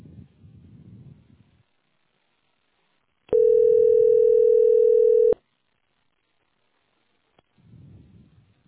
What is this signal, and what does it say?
Ringback tone (North America), 440 + 480 Hz.